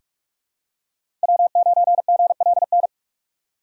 Morse code, heard as a keyed tone, about 46 wpm, 700 Hz. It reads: W9GLN
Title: USA Call Signs - 5 Characters - 6x Character Spacing - 45wpm